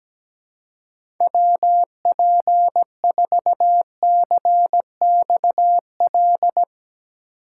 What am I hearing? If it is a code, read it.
Morse code, heard as a keyed tone, about 17 wpm, 700 Hz. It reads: WP4CXL
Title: USA Call Signs - 6 Characters - 2x Character Spacing - 17wpm